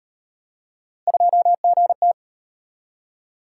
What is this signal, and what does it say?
Morse code, heard as a keyed tone, about 31 wpm, 700 Hz. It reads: 2GT
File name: Speed-Racing - Three Letter-Number Combinations - 25wpm